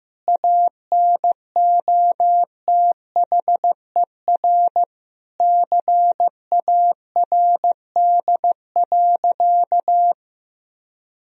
Morse code, heard as a keyed tone, about 15 wpm, 700 Hz. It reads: ANOTHER CARD.
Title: Sentences from Top 1000 Words - Review Part 4 - 15wpm